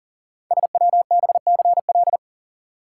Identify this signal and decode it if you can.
Morse code, heard as a keyed tone, about 38 wpm, 700 Hz. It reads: SWBCL
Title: Code Groups - 5 Letters - 40wpm